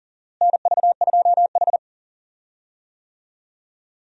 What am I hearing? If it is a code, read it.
Morse code, heard as a keyed tone, about 40 wpm, 700 Hz. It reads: NV2H
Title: USA Call Signs - 4 Characters - Repeated 2x - 40wpm